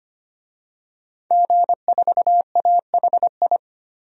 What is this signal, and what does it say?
Morse code, heard as a keyed tone, about 25 wpm, 700 Hz. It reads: G4AHI
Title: DXCC-100 Call Signs - 5 Characters - 25wpm